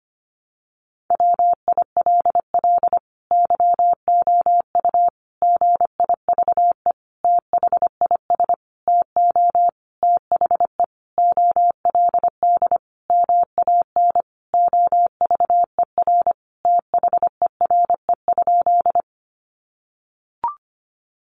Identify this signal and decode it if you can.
Morse code, heard as a keyed tone, about 25 wpm, 700 Hz. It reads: WILL YOU GIVE THIS TO THE OLD MAN OVER THERE?
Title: Sentences from Top 200 Words 25wpm